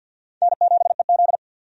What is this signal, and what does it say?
Morse code, heard as a keyed tone, about 47 wpm, 700 Hz. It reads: N7EZ
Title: USA Call Signs - 4 Characters - 50wpm